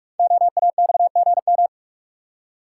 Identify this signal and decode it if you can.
Morse code, heard as a keyed tone, about 47 wpm, 700 Hz. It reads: OAXGM
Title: Code Groups - 5 Letters - 45wpm